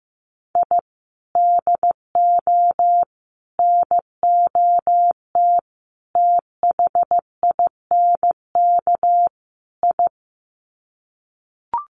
Morse code, heard as a keyed tone, about 15 wpm, 700 Hz. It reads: I DO NOT THINK I
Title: Sets of 5 Words 15wpm